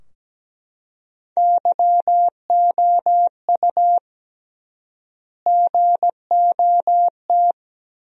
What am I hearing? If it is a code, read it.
Morse code, heard as a keyed tone, about 17 wpm, 700 Hz. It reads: YOU GOT
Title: 3x Word-Spacing - Sentences from Top 300 Words - 17wpm